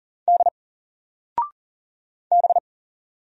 Morse code, heard as a keyed tone, about 40 wpm, 700 Hz. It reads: D B
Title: Single Character (Letter:Number:Punctuation:Prosign) 40wpm